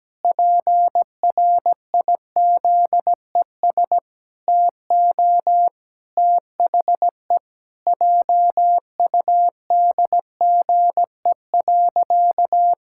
Morse code, heard as a keyed tone, about 17 wpm, 700 Hz. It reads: PRIZES TO THE JUDGE.
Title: Pangram Challenge v2 - 17wpm